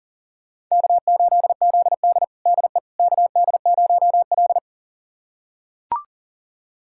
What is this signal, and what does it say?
Morse code, heard as a keyed tone, about 40 wpm, 700 Hz. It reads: K8ZD DE KD0L
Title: USA Callsign DE Callsign - 4 Characters - 40wpm